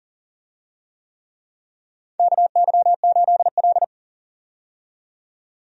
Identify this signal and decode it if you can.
Morse code, heard as a keyed tone, about 40 wpm, 700 Hz. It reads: KY8L
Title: USA Call Signs - 4 Characters - 40wpm